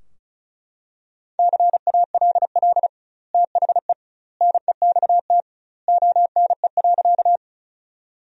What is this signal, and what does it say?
Morse code, heard as a keyed tone, about 36 wpm, 700 Hz. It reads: CALL THE NEXT ONE.
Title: 2x Word-Spacing - Sentences from Top 300 Words - 35wpm